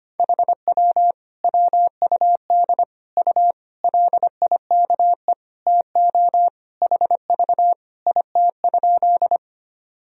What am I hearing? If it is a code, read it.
Morse code, heard as a keyed tone, about 25 wpm, 700 Hz. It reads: HW WUD U LIKE TO HV IT?
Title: Sentences from Top 100 Words and Abbreviations 25wpm